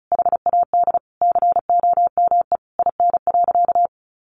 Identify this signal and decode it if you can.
Morse code, heard as a keyed tone, about 34 wpm, 700 Hz. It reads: HAD COME IN.
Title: Sentences from Top 1000 Words 35wpm